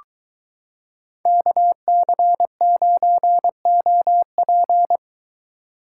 Morse code, heard as a keyed tone, about 22 wpm, 700 Hz. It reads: KC9OP